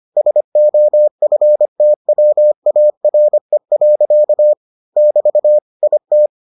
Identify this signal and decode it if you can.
Morse code, heard as a keyed tone, about 25 wpm, 600 Hz. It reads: SOFTWARE. <BT> IT